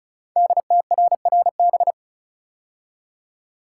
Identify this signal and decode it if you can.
Morse code, heard as a keyed tone, about 36 wpm, 700 Hz. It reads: DTRRB
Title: Code Groups - 5 Letters - 35wpm